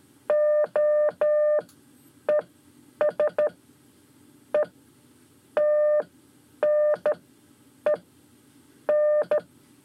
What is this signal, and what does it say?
Morse code, heard as a keyed tone, about 13 wpm, 600 Hz. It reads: O E S E T N E N
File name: 5 Lezione CW 1